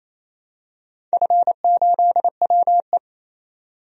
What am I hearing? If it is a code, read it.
Morse code, heard as a keyed tone, about 28 wpm, 700 Hz. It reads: F8WE
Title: DXCC-100 Call Signs - 4 Characters - 28wpm